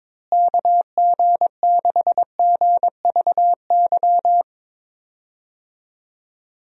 Morse code, heard as a keyed tone, about 22 wpm, 700 Hz. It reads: KG6GVY
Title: USA Call Signs - 6 Characters - Repeated 3x - 22wpm